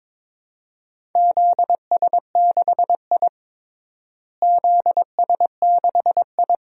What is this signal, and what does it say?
Morse code, heard as a keyed tone, about 21 wpm, 700 Hz. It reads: ZS6I ZS6I
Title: DXCC-100 Call Signs - 4 Characters - Repeated 3x - 22wpm